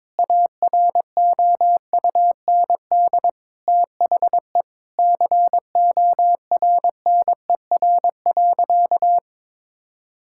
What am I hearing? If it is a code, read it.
Morse code, heard as a keyed tone, about 22 wpm, 700 Hz. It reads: AROUND THE CORNER.
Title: Sentences from Top 1600 Words 22wpm